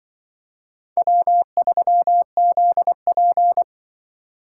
Morse code, heard as a keyed tone, about 24 wpm, 700 Hz. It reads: W3ZP
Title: Speed Racing - USA 1x2 Call Signs 20wpm